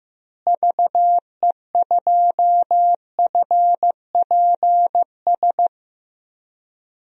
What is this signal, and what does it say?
Morse code, heard as a keyed tone, about 15 wpm, 700 Hz. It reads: VE2FPS
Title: DXCC-100 Call Signs - 6 Characters - Repeated 3x - 15wpm